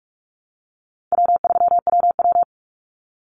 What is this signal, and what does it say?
Morse code, heard as a keyed tone, about 44 wpm, 700 Hz. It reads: W3WW
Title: Easy Call Signs Repeated 2x - 45wpm